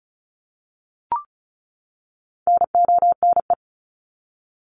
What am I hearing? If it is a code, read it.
Morse code, heard as a keyed tone, about 34 wpm, 700 Hz. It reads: NONE
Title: Top 1501-1750 Words 35wpm